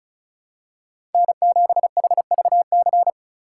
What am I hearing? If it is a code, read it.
Morse code, heard as a keyed tone, about 36 wpm, 700 Hz. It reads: N7HVC
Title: USA Call Signs - 5 Characters - 3x Character Spacing - 35wpm